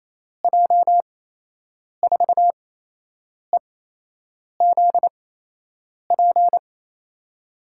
Morse code, heard as a keyed tone, about 28 wpm, spaced as for 8 wpm, 700 Hz. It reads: J4EZP
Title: USA Call Signs - 8x Character Spacing - 28wpm